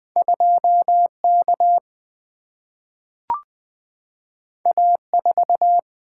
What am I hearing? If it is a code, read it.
Morse code, heard as a keyed tone, about 20 wpm, 700 Hz. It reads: 2K A4
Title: Single-Letter Single-Number 20wpm